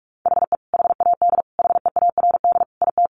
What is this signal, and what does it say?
Morse code, heard as a keyed tone, about 44 wpm, 700 Hz. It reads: HE HAD HEARD IT
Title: Sentences from Top 1400 Words 45wpm